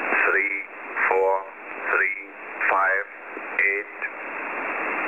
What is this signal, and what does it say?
Emergency Alert System (SAME) tones, heard as harsh buzzing bursts, with speech. It is heard from 0.3 to 3.9 s.